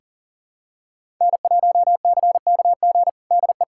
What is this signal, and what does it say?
Morse code, heard as a keyed tone, about 37 wpm, 700 Hz. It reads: N1CKG DE
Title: USA Callsign DE Callsign - 5 Characters - 40wpm